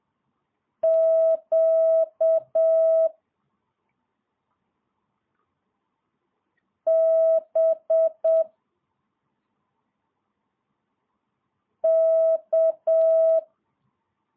Morse code, heard as a keyed tone, about 7 wpm, 650 Hz. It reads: Q B K
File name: jh8cbh / 2022cw